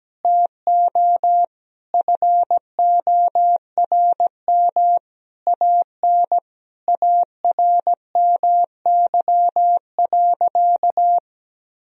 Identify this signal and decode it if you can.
Morse code, heard as a keyed tone, about 17 wpm, 700 Hz. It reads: TO FORM AN ARMY.
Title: Sentences from Top 1000 Words - Review Part 1 - 17wpm